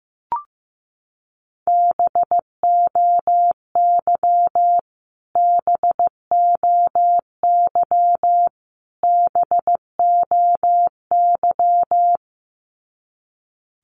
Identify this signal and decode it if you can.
Morse code, heard as a keyed tone, about 15 wpm, 700 Hz. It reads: BOY BOY BOY
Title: Top 200 Words Repeated 15wpm